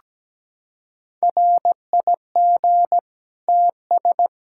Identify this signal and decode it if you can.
Morse code, heard as a keyed tone, about 17 wpm, 700 Hz. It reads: RIG TS
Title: Practice QSO - Exchange 2 - Info - 17wpm